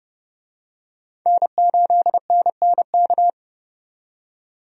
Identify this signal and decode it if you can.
Morse code, heard as a keyed tone, about 32 wpm, 700 Hz. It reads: N8NNK
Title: USA Call Signs - 5 Characters - 30wpm